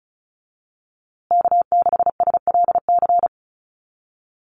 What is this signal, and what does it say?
Morse code, heard as a keyed tone, about 34 wpm, 700 Hz. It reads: K6SLC
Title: USA Call Signs Repeated 2x 35wpm